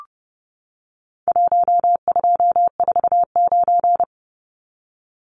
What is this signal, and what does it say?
Morse code, heard as a keyed tone, about 30 wpm, 700 Hz. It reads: 1249
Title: Numbers - 4-Digit 30wpm